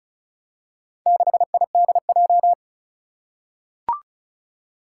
Morse code, heard as a keyed tone, about 34 wpm, 700 Hz. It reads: 6IDJ
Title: Four Challenging Letter-Number Combinations 35wpm